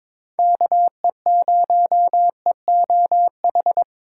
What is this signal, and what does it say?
Morse code, heard as a keyed tone, about 21 wpm, 700 Hz. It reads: KE0EOH
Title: USA Call Signs - 6 Characters - 3x Character Spacing - 22wpm